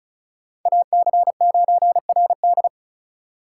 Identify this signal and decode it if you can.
Morse code, heard as a keyed tone, about 44 wpm, 700 Hz. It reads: AC9RD